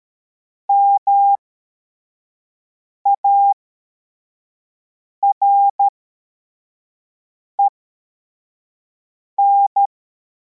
Morse code, heard as a keyed tone, about 13 wpm, spaced as for 4 wpm, 800 Hz. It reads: MAREN
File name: formation / cw / Text25